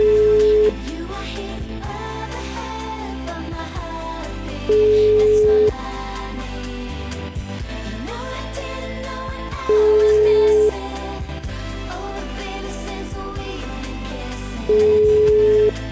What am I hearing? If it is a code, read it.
Ringback tone (Europe, 425 Hz).